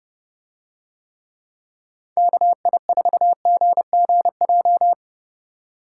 Morse code, heard as a keyed tone, about 30 wpm, 700 Hz. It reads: KI4GGJ